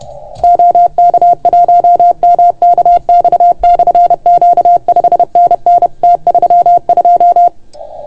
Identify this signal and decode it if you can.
Morse code, heard as a keyed tone, about 31 wpm, 690 Hz. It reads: OK1MKX/Q 5NN T32
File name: Archiv / CUC381 / OK1FGD CUC 381